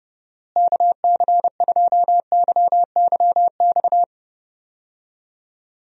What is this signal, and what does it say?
Morse code, heard as a keyed tone, about 32 wpm, 700 Hz. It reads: KC2YYX
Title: USA Call Signs - 6 Characters - Repeated 2x - 30wpm